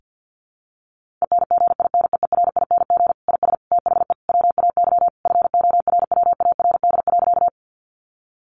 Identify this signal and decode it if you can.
Morse code, heard as a keyed tone, about 50 wpm, 700 Hz. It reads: ENGINEERING IS THE WAY FORWARD.
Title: Sentences from Top 1500 Words 50wpm